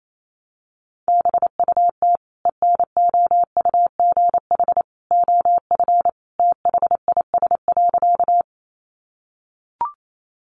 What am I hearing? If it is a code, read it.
Morse code, heard as a keyed tone, about 28 wpm, 700 Hz. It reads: BUT ENOUGH OF THIS.
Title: Sentences from Top 400 Words 28wpm